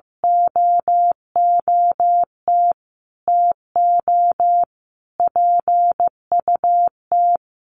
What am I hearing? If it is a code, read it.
Morse code, heard as a keyed tone, about 14 wpm, 700 Hz. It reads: OOT TO PUT
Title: Sentences from Top 1300 Words 15wpm